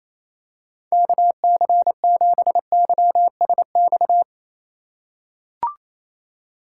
Morse code, heard as a keyed tone, about 27 wpm, 700 Hz. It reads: KC7YSX
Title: USA Call Signs - 6 Characters - 3x Character Spacing - 28wpm